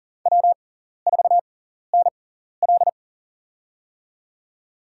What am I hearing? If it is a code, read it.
Morse code, heard as a keyed tone, about 40 wpm, spaced as for 14 wpm, 700 Hz. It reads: W4NL
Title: USA Call Signs - 6x Character Spacing - 40wpm